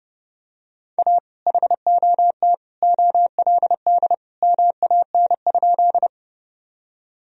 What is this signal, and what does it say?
Morse code, heard as a keyed tone, about 30 wpm, 700 Hz. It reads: A HOT OLD MAN?
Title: Sentences from Top 1000 Words - Review Part 3 - 30wpm